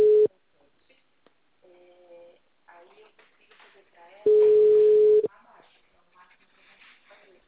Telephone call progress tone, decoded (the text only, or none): ringback tone